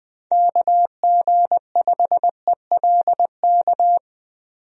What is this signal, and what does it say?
Morse code, heard as a keyed tone, about 20 wpm, 700 Hz. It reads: KG5ELK
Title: USA Call Signs - 6 Characters - 20wpm